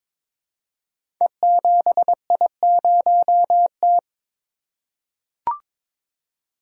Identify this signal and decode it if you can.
Morse code, heard as a keyed tone, about 22 wpm, 700 Hz. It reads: E7I0T
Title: Code Groups - 5 Letter:Numbers - 22wpm